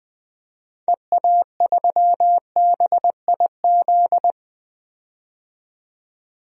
Morse code, heard as a keyed tone, about 20 wpm, 700 Hz. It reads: EA3BIZ
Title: DXCC-100 Call Signs - 6 Characters - 20wpm